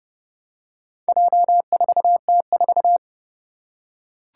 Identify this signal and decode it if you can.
Morse code, heard as a keyed tone, about 30 wpm, 700 Hz. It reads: J4T4
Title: Four Challenging Letter-Number Combinations 30wpm